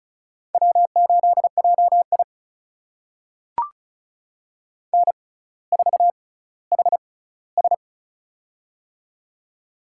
Morse code, heard as a keyed tone, about 36 wpm, spaced as for 18 wpm, 700 Hz. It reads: W8JI N4HS
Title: USA Call Signs - 4 Characters - 6x Character Spacing - 35wpm